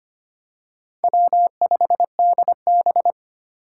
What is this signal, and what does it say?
Morse code, heard as a keyed tone, about 25 wpm, 700 Hz. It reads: W5DB
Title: USA Call Signs - 4 Characters - 4x Character Spacing - 25wpm